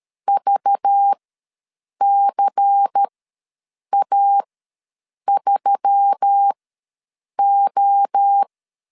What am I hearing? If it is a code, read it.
Morse code, heard as a keyed tone, about 13 wpm, spaced as for 7 wpm, 800 Hz. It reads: VCA3O
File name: LEZIONI / 9 28 ws 12wpm 3354